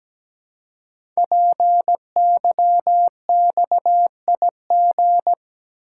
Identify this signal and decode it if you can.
Morse code, heard as a keyed tone, about 17 wpm, 700 Hz. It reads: PYXIG
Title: Code Groups - 5 Letters - 17wpm